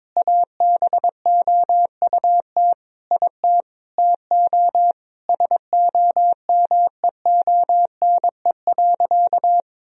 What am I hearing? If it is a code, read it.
Morse code, heard as a keyed tone, about 22 wpm, 700 Hz. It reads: ABOUT IT TO SOMEONE.
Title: Sentences from Top 1000 Words - Review Part 4 - 22wpm